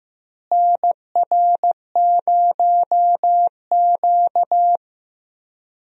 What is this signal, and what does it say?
Morse code, heard as a keyed tone, about 16 wpm, 700 Hz. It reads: NR0Q
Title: USA Call Signs - 4 Characters - Repeated 2x - 15wpm